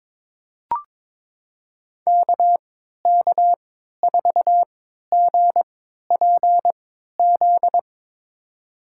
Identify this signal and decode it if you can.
Morse code, heard as a keyed tone, about 22 wpm, spaced as for 13 wpm, 700 Hz. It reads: KK4GPZ
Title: USA Call Signs - 6 Characters - 3x Character Spacing - 22wpm